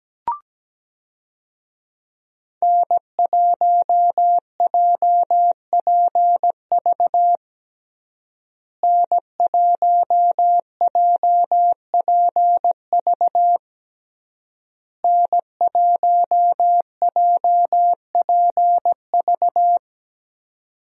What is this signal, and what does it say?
Morse code, heard as a keyed tone, about 17 wpm, 700 Hz. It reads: N1JPV N1JPV N1JPV
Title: USA Call Signs - 5 Characters - Repeated 3x - 17wpm